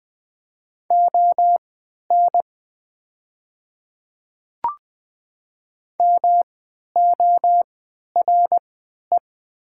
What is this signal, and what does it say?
Morse code, heard as a keyed tone, about 20 wpm, spaced as for 11 wpm, 700 Hz. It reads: ON MORE
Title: 3x Character Spacing - Top 100 Words - 20wpm